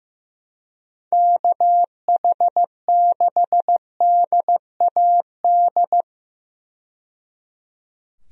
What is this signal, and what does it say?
Morse code, heard as a keyed tone, about 15 wpm, 700 Hz. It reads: KH6DAD
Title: DXCC-100 Call Signs - 6 Characters - 15wpm